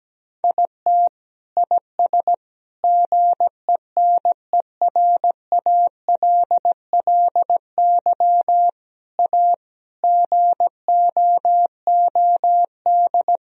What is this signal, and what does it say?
Morse code, heard as a keyed tone, about 17 wpm, 700 Hz. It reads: IT IS GENERALLY A GOOD
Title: Sentences from Top 1000 Words - Review Part 4 - 17wpm